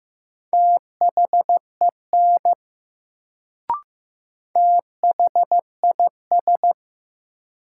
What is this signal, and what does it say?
Morse code, heard as a keyed tone, about 15 wpm, 700 Hz. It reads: THEN THIS
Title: Sentences from Top 100 Words - New Word Drill - 15wpm